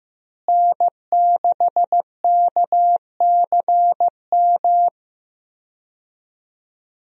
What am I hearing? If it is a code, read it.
Morse code, heard as a keyed tone, about 15 wpm, 700 Hz. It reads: N6KCM
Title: USA Call Signs - 5 Characters - Repeated 3x - 15wpm